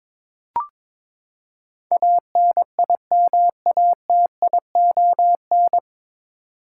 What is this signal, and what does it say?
Morse code, heard as a keyed tone, about 22 wpm, 700 Hz. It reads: ANIMATION
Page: Morse Code Ninja practice 22wpm